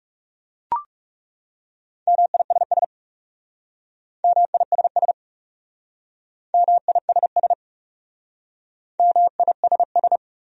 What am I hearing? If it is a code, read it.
Morse code, heard as a keyed tone, about 36 wpm, 700 Hz. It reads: MISS MISS MISS MISS
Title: Speed-Racing Top 500 Words 30wpm